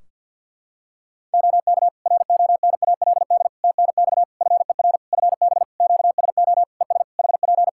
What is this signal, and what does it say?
Morse code, heard as a keyed tone, about 51 wpm, 700 Hz. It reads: OK RONALD TNX FER FB QSO ES HP
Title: Practice QSO - Exchange 4 - Ending - 50wpm